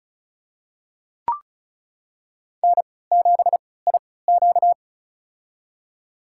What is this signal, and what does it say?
Morse code, heard as a keyed tone, about 36 wpm, spaced as for 20 wpm, 700 Hz. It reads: N7IQ
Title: USA Call Signs - 3x Character Spacing - 35wpm